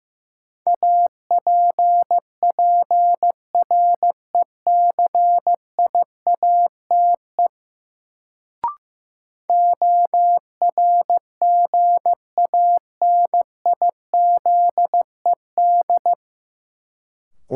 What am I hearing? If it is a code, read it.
Morse code, heard as a keyed tone, about 15 wpm, 700 Hz. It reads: APPRECIATE ORGANIZED
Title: Sentences from Top 1300 Words - New Word Drill - 15wpm